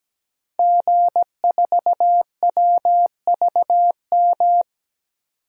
Morse code, heard as a keyed tone, about 17 wpm, 700 Hz. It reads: G4WVM
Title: DXCC-100 Call Signs - 5 Characters - 17wpm